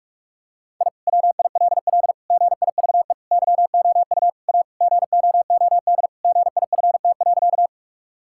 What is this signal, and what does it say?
Morse code, heard as a keyed tone, about 46 wpm, 700 Hz. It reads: I WILL GIVE YOU A GOOD GIFT.
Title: Sentences from Top 1000 Words - Review Part 6 - 45wpm